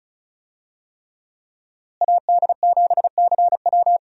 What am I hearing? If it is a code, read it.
Morse code, heard as a keyed tone, about 36 wpm, 700 Hz. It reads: AD7CW
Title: USA Call Signs - 5 Characters - Repeated 3x - 35wpm